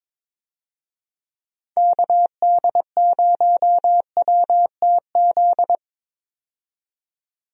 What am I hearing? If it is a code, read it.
Morse code, heard as a keyed tone, about 22 wpm, 700 Hz. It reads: KD0WTZ
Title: USA Call Signs - 6 Characters - 22wpm